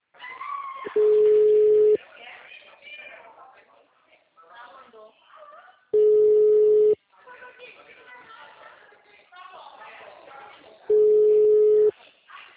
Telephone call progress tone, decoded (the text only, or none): ringback tone